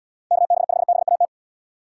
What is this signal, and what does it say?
Morse code, heard as a keyed tone, about 67 wpm, 700 Hz. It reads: KB5BAT